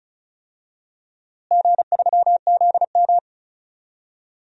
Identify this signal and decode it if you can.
Morse code, heard as a keyed tone, about 36 wpm, 700 Hz. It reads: G3ZM
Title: DXCC-100 Call Signs - 4 Characters - Repeated 3x - 35wpm